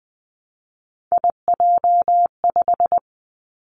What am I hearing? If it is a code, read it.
Morse code, heard as a keyed tone, about 20 wpm, 700 Hz. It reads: IJ5
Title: Three Challenging Letter-Number Combinations 20wpm